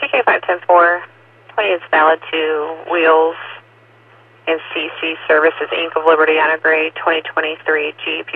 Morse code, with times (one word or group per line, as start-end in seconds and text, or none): none